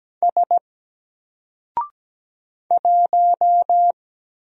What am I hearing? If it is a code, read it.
Morse code, heard as a keyed tone, about 17 wpm, 700 Hz. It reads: S 1
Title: Single Character (Letter:Number:Punctuation:Prosign) 17wpm